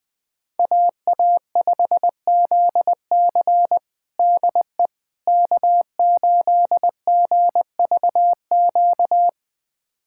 Morse code, heard as a keyed tone, about 20 wpm, 700 Hz. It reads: AA5ZC DE K8GVQ
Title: USA Callsign DE Callsign - 5 Characters - 20wpm